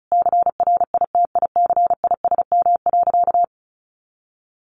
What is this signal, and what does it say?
Morse code, heard as a keyed tone, about 34 wpm, 700 Hz. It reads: CRITICISM.
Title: Sentences from Top 1300 Words 35wpm